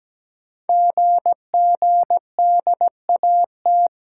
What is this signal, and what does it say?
Morse code, heard as a keyed tone, about 17 wpm, 700 Hz. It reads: GGDAT
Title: Code Groups - 5 Letters - 17wpm